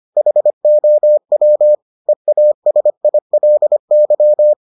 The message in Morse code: HOW EASILY

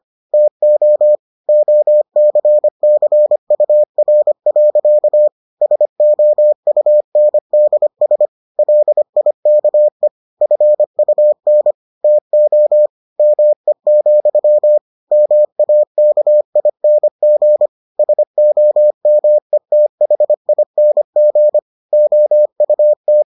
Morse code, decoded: TO OCCUR. SOUNDS LIKE FUN TO ME, MAKING SOMETHING OUT

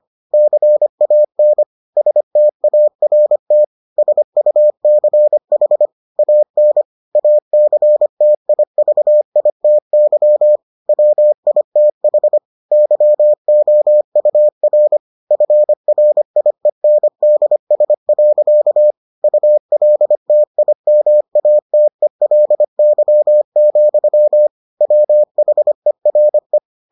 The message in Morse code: CAN START SUCH AN ACTIVITY WITH YOUR FRIENDS. ULTIMATELY, WHERE

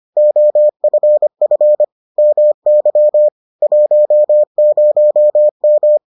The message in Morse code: OFF MY 10M

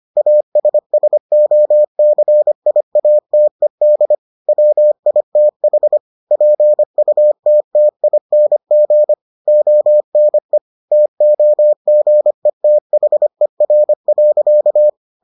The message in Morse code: ASSOCIATED WITH PUTTING ONE TOGETHER.

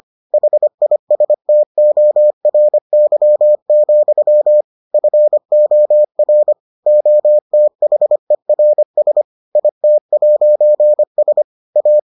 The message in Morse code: HISTORY, FOR OTHERS IT'S A

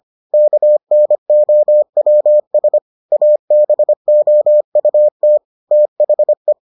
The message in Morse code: KNOWS ABOUT THE

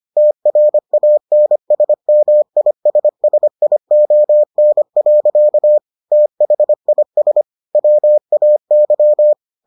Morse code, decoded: TRANSMISSION. THIS WAY

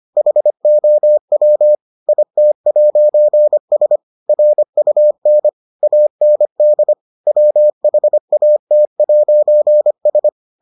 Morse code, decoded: HOW IT'S RUN AND WHAT'S

